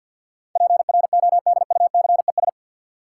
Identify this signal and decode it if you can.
Morse code, heard as a keyed tone, about 51 wpm, 700 Hz. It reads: PRODUCES
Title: Sentences from Top 1000 Words - Review Part 6 - New Word Drill - 50wpm